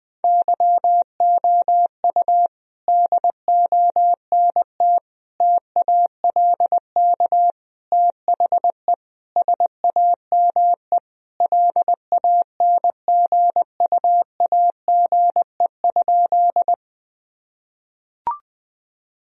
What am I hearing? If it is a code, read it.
Morse code, heard as a keyed tone, about 20 wpm, 700 Hz. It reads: YOU DONT TALK THE SAME LANGUAGE?
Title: Sentences from Top 1000 Words - Review Part 3 - 20wpm